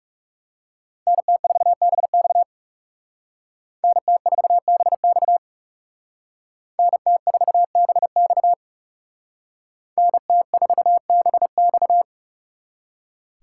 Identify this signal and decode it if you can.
Morse code, heard as a keyed tone, about 36 wpm, 700 Hz. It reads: NT4BX NT4BX NT4BX NT4BX
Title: Speed Racing - USA 2x2 Call Signs 30wpm